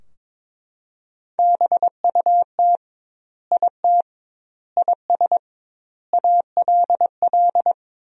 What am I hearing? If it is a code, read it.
Morse code, heard as a keyed tone, about 22 wpm, 700 Hz. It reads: BUT IT IS ALL
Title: 2x Word-Spacing - Sentences from Top 300 Words - 22wpm